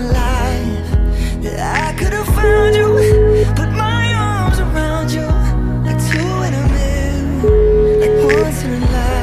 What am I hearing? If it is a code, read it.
Ringback tone (Europe, 425 Hz), with noise.